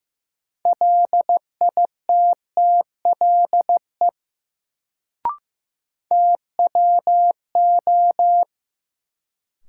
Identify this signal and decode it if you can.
Morse code, heard as a keyed tone, about 15 wpm, 700 Hz. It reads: LITTLE TWO
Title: Sets of 2 Words - New Word Drill - 15wpm